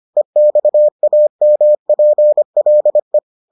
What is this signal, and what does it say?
Morse code, heard as a keyed tone, about 25 wpm, 600 Hz. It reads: EXAMPLE